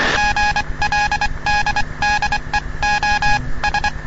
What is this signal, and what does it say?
Morse code, heard as a keyed tone, about 24 wpm, spaced as for 21 wpm, 830 Hz, with noise. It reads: GLDDEOS
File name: europe / bel / bel-265-ostooestenderadioclg-g-240-gldlsndsend-1997-ll